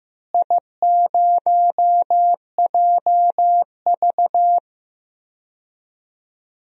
Morse code, heard as a keyed tone, about 15 wpm, 700 Hz. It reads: I0JV
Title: DXCC-100 Call Signs - 4 Characters - Repeated 2x - 15wpm